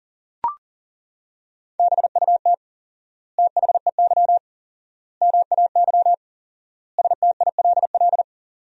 Morse code, heard as a keyed tone, about 40 wpm, 700 Hz. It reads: BUT THEY MAY STILL
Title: 4x Word-Spacing - Sentences from Top 300 Words - 40wpm